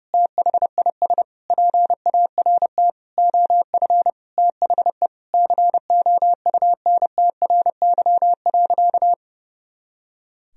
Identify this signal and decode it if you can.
Morse code, heard as a keyed tone, about 30 wpm, 700 Hz. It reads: THIS PART OF THE COUNTRY.
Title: Sentences from Top 300 Words and Abbreviations 30wpm